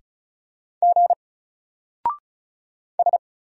Morse code, heard as a keyed tone, about 36 wpm, 700 Hz. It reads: G S
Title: Single Letter-Number 35wpm